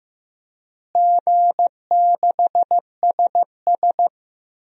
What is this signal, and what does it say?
Morse code, heard as a keyed tone, about 16 wpm, 700 Hz. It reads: G6SS
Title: DXCC-100 Call Signs - 4 Characters - Repeated 3x - 15wpm